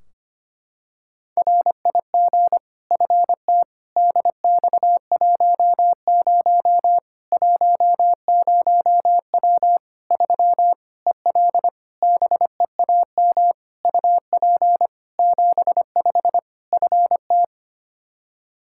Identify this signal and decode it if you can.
Morse code, heard as a keyed tone, about 25 wpm, 700 Hz. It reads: RIG FT DX10 10W 3 EL BEAM UP 75 FT
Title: K1USN SST QSO Practice 25wpm